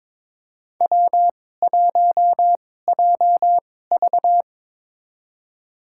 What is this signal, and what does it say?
Morse code, heard as a keyed tone, about 22 wpm, spaced as for 16 wpm, 700 Hz. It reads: W1JV
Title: USA Call Signs - 4 Characters - 2x Character Spacing - 22wpm